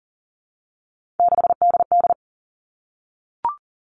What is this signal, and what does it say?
Morse code, heard as a keyed tone, about 40 wpm, 700 Hz. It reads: 6DD
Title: Three Challenging Letter-Number Combinations 40wpm